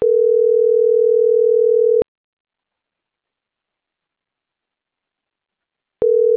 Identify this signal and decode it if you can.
Ringback tone (North America), 440 + 480 Hz.